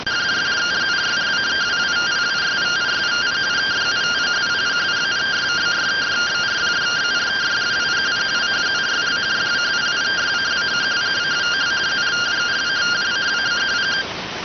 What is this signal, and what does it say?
Two-tone teleprinter signal, about 170 Hz shift.